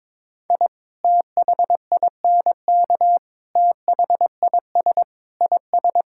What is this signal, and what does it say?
Morse code, heard as a keyed tone, about 22 wpm, 700 Hz. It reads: I THINK THIS IS
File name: practice / 22wpm / Sentences with 4-Word Phrases - 22wpm